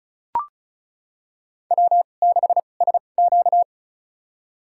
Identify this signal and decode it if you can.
Morse code, heard as a keyed tone, about 34 wpm, spaced as for 25 wpm, 700 Hz. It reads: W6SQ
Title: USA Call Signs - 4 Characters - 2x Character Spacing - 35wpm